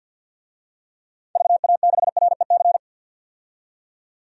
Morse code, heard as a keyed tone, about 52 wpm, 700 Hz. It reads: VA6REC
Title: DXCC-100 Call Signs - 6 Characters - 50wpm